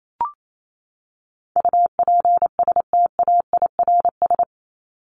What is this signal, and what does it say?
Morse code, heard as a keyed tone, about 28 wpm, 700 Hz. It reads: UPSTAIRS
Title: Top 1751-2000 Words 28wpm